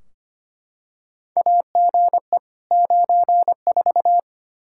Morse code, heard as a keyed tone, about 25 wpm, 700 Hz. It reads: AGE 94
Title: Practice QSO - Exchange 3 - More Info - 25wpm